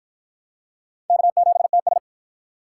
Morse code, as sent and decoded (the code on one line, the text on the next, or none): -.- --... - ...
K7TS